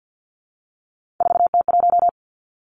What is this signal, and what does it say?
Morse code, heard as a keyed tone, about 50 wpm, 700 Hz. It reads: 4T1
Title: Three Challenging Letter-Number Combinations 50wpm